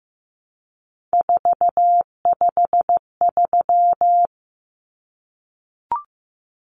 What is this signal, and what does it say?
Morse code, heard as a keyed tone, about 15 wpm, 700 Hz. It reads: 453
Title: Common Cut Numbers - 3-Digit 15wpm